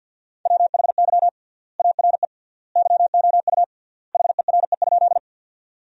Morse code, heard as a keyed tone, about 46 wpm, 700 Hz. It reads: WHY ARE YOU HERE?
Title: 3x Word-Spacing - Sentences from Top 200 Words - 50wpm